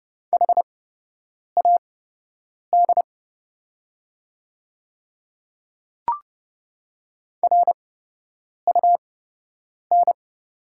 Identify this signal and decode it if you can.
Morse code, heard as a keyed tone, about 30 wpm, spaced as for 8 wpm, 700 Hz. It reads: HAD RUN
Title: 8x Character Spacing - Top 500 Words - 30wpm